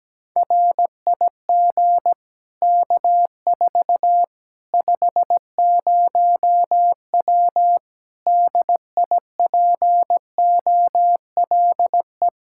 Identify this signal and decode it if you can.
Morse code, heard as a keyed tone, about 17 wpm, 700 Hz. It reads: RIG K4 50W DIPOLE
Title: Practice QSO - Exchange 2 - Info - 17wpm